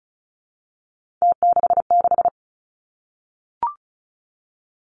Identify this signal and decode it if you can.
Morse code, heard as a keyed tone, about 36 wpm, 700 Hz. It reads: T66
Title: Three Challenging Letter-Number Combinations 35wpm